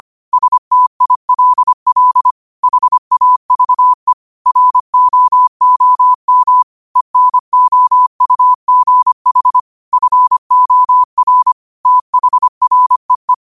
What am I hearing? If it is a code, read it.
Morse code, heard as a keyed tone, about 25 wpm, 1000 Hz. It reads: STILL HAVE ROOM ENOUGH FOR THREE
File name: qotdfiles / 25wpm